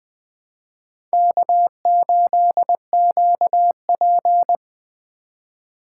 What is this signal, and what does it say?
Morse code, heard as a keyed tone, about 25 wpm, 700 Hz. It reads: K8QP
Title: Speed Racing - USA 1x2 Call Signs 20wpm